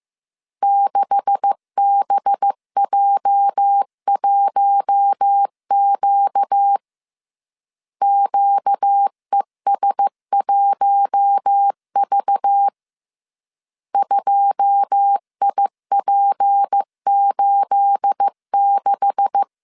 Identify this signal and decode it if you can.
Morse code, heard as a keyed tone, about 15 wpm, 800 Hz. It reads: 6BJ1Q QES1V 2IP86
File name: CW / DET / 1 9 1500C S A 14wpm 3222